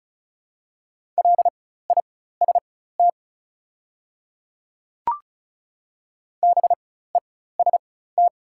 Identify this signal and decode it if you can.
Morse code, heard as a keyed tone, about 36 wpm, spaced as for 16 wpm, 690 Hz. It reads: LIST BEST
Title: 4x Character Spacing - Top 500 Words - 35wpm